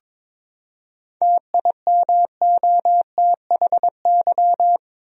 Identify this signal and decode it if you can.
Morse code, heard as a keyed tone, about 22 wpm, 700 Hz. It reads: TIMOTHY
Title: USA Common Names - 22wpm